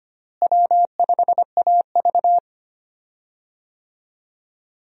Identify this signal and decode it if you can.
Morse code, heard as a keyed tone, about 25 wpm, 700 Hz. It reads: W5AV
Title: USA Call Signs - 4 Characters - Repeated 3x - 25wpm